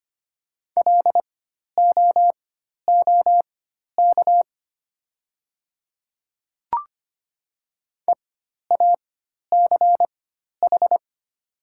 Morse code, heard as a keyed tone, about 25 wpm, spaced as for 12 wpm, 700 Hz. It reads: LOOK EACH
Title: 4x Character Spacing - Top 100 Words - 25wpm